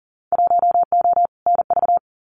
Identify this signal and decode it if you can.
Morse code, heard as a keyed tone, about 41 wpm, 700 Hz. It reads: 1O NV
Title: Winter Field Day Exchanges Encore 40wpm